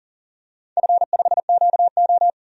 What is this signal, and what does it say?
Morse code, heard as a keyed tone, about 40 wpm, 700 Hz. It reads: F5QO
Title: DXCC-100 Call Signs - 4 Characters - 40wpm